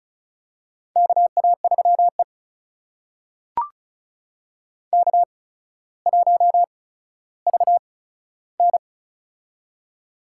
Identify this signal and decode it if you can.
Morse code, heard as a keyed tone, about 36 wpm, spaced as for 15 wpm, 700 Hz. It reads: KA3E K1VN
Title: USA Call Signs - 4 Characters - 8x Character Spacing - 35wpm